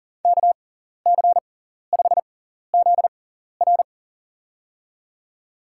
Morse code, heard as a keyed tone, about 40 wpm, spaced as for 14 wpm, 700 Hz. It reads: KC5ZR
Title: USA Call Signs - 6x Character Spacing - 40wpm